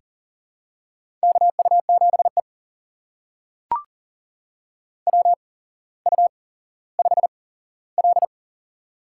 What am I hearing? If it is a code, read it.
Morse code, heard as a keyed tone, about 40 wpm, spaced as for 17 wpm, 700 Hz. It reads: KU7E WU5L
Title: USA Call Signs - 8x Character Spacing - 40wpm